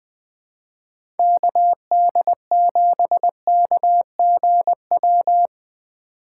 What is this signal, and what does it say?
Morse code, heard as a keyed tone, about 20 wpm, 700 Hz. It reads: KD7KGW
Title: USA Call Signs - 4x Character Spacing - 20wpm